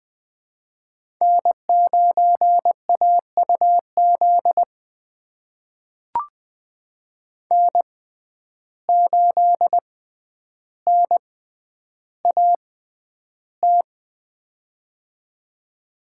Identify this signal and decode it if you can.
Morse code, heard as a keyed tone, about 20 wpm, spaced as for 10 wpm, 700 Hz. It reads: N9AUZ N8NAT
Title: USA Call Signs - 5 Characters - 6x Character Spacing - 20wpm